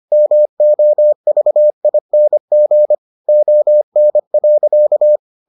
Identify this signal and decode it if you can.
Morse code, heard as a keyed tone, about 25 wpm, 600 Hz. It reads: MOVING ON.